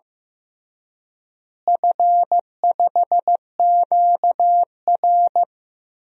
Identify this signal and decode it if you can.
Morse code, heard as a keyed tone, about 15 wpm, 700 Hz. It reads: F5QR
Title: DXCC-100 Call Signs - 4 Characters - Repeated 2x - 15wpm